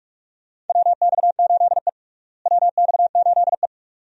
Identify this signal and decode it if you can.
Morse code, heard as a keyed tone, about 45 wpm, 700 Hz. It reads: WX8E WX8E
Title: USA Call Signs - 4 Characters - Repeated 2x - 45wpm